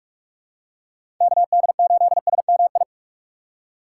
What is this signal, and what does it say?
Morse code, heard as a keyed tone, about 42 wpm, 700 Hz. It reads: KD8SMI
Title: USA Call Signs - 6 Characters - 3x Character Spacing - 45wpm